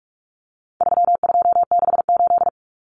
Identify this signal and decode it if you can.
Morse code, heard as a keyed tone, about 44 wpm, 700 Hz. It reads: 3268
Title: Numbers - 4-Digit 45wpm